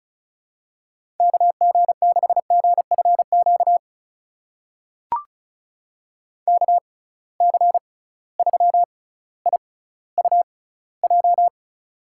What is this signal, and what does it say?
Morse code, heard as a keyed tone, about 36 wpm, spaced as for 18 wpm, 700 Hz. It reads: KG6GFQ KC3IUJ
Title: USA Call Signs - 6 Characters - 6x Character Spacing - 35wpm